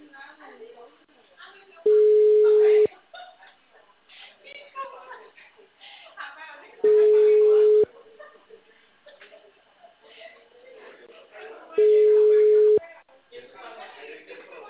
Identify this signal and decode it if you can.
Ringback tone (Europe, 425 Hz), with speech.